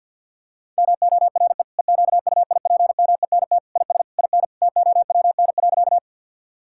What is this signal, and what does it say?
Morse code, heard as a keyed tone, about 50 wpm, 700 Hz. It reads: MORE EQUIPMENT IS IN TOWN.
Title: Sentences from Top 1000 Words - Review Part 5 - 50wpm